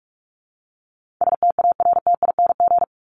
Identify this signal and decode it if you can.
Morse code, heard as a keyed tone, about 44 wpm, 700 Hz. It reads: STARTING
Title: ING Suffix - Top 100 Words with Base Words 45wpm